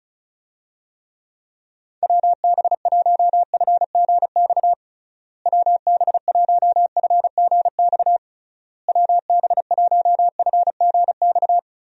Morse code, heard as a keyed tone, about 36 wpm, 700 Hz. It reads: WB1FGX WB1FGX WB1FGX
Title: USA Call Signs - 6 Characters - Repeated 3x - 35wpm